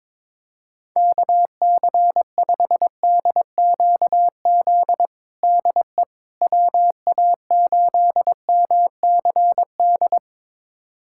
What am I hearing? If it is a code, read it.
Morse code, heard as a keyed tone, about 22 wpm, 700 Hz. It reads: KC5DQZ DE WA8MCD
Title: USA Callsign DE Callsign 22wpm